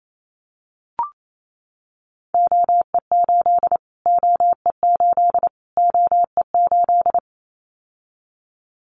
Morse code, heard as a keyed tone, about 28 wpm, 700 Hz. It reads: OE8 OE8 OE8
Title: Callsign Prefixes for Top 25 DXCC - Repeated 3x - 28wpm